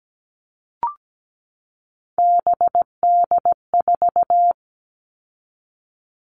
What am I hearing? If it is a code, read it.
Morse code, heard as a keyed tone, about 17 wpm, 700 Hz. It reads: BD4
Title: Callsign Prefixes for Top 100 DXCC - 17wpm